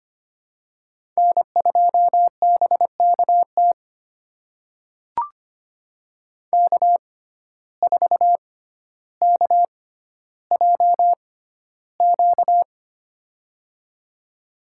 Morse code, heard as a keyed tone, about 25 wpm, spaced as for 13 wpm, 700 Hz. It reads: N2BKT K4KJQ
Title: USA Call Signs - 5 Characters - 6x Character Spacing - 25wpm